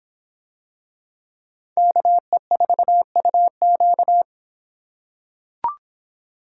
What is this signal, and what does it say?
Morse code, heard as a keyed tone, about 24 wpm, 700 Hz. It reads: KE4UQ